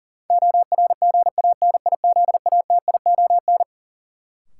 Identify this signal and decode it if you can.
Morse code, heard as a keyed tone, about 40 wpm, 700 Hz. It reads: ORGANIZATION
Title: Sentences from Top 1100 Words - New Word Drill Alt - 40wpm